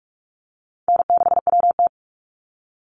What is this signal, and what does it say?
Morse code, heard as a keyed tone, about 46 wpm, 700 Hz. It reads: N6WT
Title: NA SOTA Chaser Callsigns 45wpm